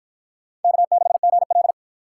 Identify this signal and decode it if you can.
Morse code, heard as a keyed tone, about 53 wpm, 700 Hz. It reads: K6GL